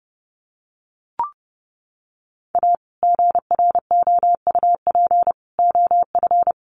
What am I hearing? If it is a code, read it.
Morse code, heard as a keyed tone, about 30 wpm, 700 Hz. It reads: A GROUP OF
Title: Sets of 3 Words 30wpm